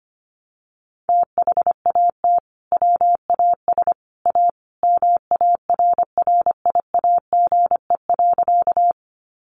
Morse code, heard as a keyed tone, about 25 wpm, 700 Hz. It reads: THAT WAS A MARRIAGE.
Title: Sentences from Top 800 Words 25wpm